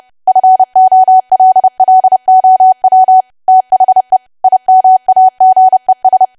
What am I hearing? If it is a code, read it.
Morse code, heard as a keyed tone, about 30 wpm, 750 Hz. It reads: FOLLOW THE IMAGES
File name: cwa / W1AW-Level3 / 30WPM